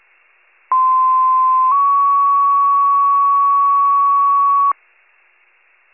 Two-tone page, heard as a one-second tone, then a three-second tone, about 1040 then 1120 Hz. It is heard from 0.7 to 4.7 s.